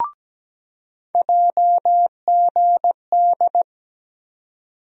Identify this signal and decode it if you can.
Morse code, heard as a keyed tone, about 17 wpm, 700 Hz. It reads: JGD
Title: USA Call Sign Postfixes - 3 Letters 17wpm